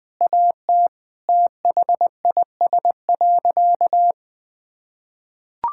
Morse code, heard as a keyed tone, about 20 wpm, 700 Hz. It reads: AT THIS.
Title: Sentences with 3-Word Phrases - Spoken Beginning - 20wpm